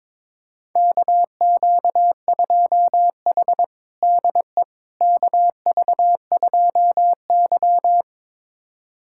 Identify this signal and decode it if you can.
Morse code, heard as a keyed tone, about 22 wpm, 700 Hz. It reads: KQ2H DE KV2Y
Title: USA Callsign DE Callsign - 4 Characters - 22wpm